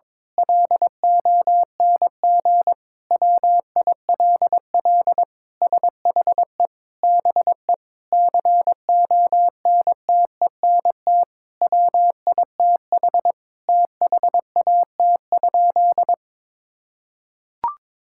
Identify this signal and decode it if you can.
Morse code, heard as a keyed tone, about 22 wpm, 700 Hz. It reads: LONG WILL SHE BE CONTENT WITH THAT?
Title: Sentences from Top 1000 Words - Review Part 6 - 22wpm